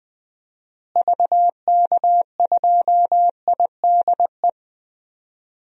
Morse code, heard as a keyed tone, about 20 wpm, 700 Hz. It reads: VK2IDE
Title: DXCC-100 Call Signs - 6 Characters - Repeated 2x - 20wpm